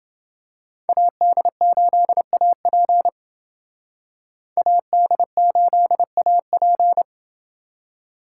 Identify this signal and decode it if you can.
Morse code, heard as a keyed tone, about 28 wpm, 700 Hz. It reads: AD8AP AD8AP